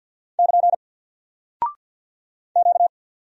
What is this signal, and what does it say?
Morse code, heard as a keyed tone, about 48 wpm, 700 Hz. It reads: <KN> Q
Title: Single Character (Letter:Number:Punctuation:Prosign) Alt 50wpm